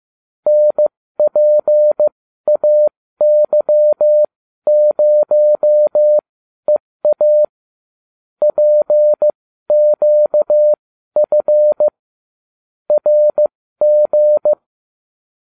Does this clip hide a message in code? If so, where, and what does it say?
Yes: Morse code from 0.5 to 14.5 s, heard as a keyed tone, about 15 wpm, spaced as for 12 wpm, 600 Hz. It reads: NPAY0EA PQF RG